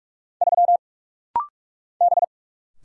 Morse code, heard as a keyed tone, about 44 wpm, 700 Hz. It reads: 3 B
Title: Single Letter-Number Alt - Rapid-Fire 45wpm